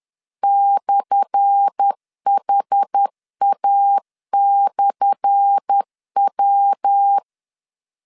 Morse code, heard as a keyed tone, about 10 wpm, 800 Hz. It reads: /HA/W
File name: LEZIONI / 10 SIGNS / 10A SLASH 10wpm 2808